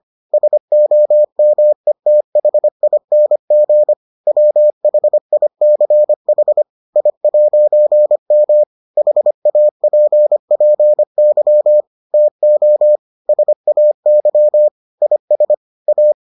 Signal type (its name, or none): Morse code